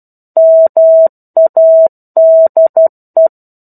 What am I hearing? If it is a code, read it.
Morse code, heard as a keyed tone, about 12 wpm, 650 Hz. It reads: MADE
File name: 01-Chapter-01 12 WPM